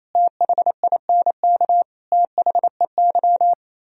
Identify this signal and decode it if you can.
Morse code, heard as a keyed tone, about 34 wpm, 700 Hz. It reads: THINK THEY
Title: Speed-Racing - Sets of 5 Words 28wpm